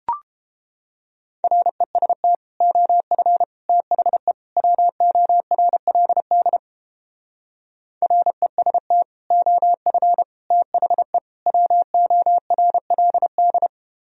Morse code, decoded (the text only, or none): REST OF THE WORLD REST OF THE WORLD